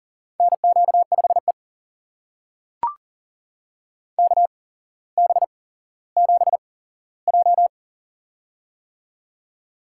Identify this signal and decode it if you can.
Morse code, heard as a keyed tone, about 39 wpm, spaced as for 17 wpm, 700 Hz. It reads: NQ5E KB7J
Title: USA Call Signs - 4 Characters - 8x Character Spacing - 40wpm